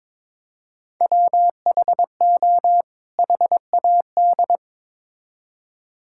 Morse code, heard as a keyed tone, about 22 wpm, 700 Hz. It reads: WHO HAD
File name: Build A Phrase - 2-Word Phrases - 22wpm